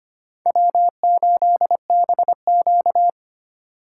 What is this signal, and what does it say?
Morse code, heard as a keyed tone, about 25 wpm, 700 Hz. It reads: W8BQ
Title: USA Call Signs - 3x Character Spacing - 25wpm